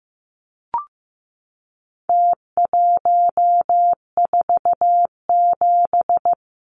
Morse code, heard as a keyed tone, about 15 wpm, 700 Hz. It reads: T147
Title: Numbers - Contest Style - 4-Digit 15wpm